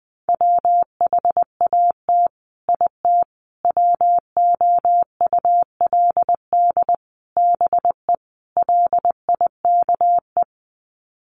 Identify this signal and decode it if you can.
Morse code, heard as a keyed tone, about 20 wpm, 700 Hz. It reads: WHAT IT WOULD BE LIKE
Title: Sets of 5 Words Encore 20wpm